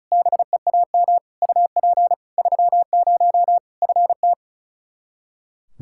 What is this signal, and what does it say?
Morse code, heard as a keyed tone, about 36 wpm, 700 Hz. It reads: BEAM UP 30 FT
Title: Practice QSO - Exchange 2 - Info - 35wpm